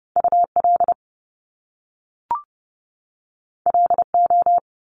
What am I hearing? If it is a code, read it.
Morse code, heard as a keyed tone, about 30 wpm, 700 Hz. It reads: UL LO
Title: Top Two Letter Combinations in Words 30wpm